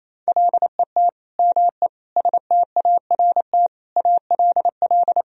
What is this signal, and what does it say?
Morse code, heard as a keyed tone, about 28 wpm, 700 Hz. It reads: LET ME START ALL
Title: Sentences from Top 1000 Words - Review Part 1 - 28wpm